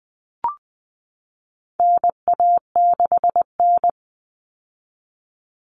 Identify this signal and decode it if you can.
Morse code, heard as a keyed tone, about 20 wpm, 700 Hz. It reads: NA6N
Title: Common Cut Numbers - 4-Digit 20wpm